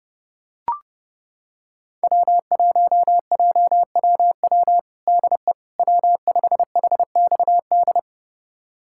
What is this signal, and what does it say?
Morse code, heard as a keyed tone, about 32 wpm, 700 Hz. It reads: W1JWW DE W5HXD
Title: USA Callsign DE Callsign 30wpm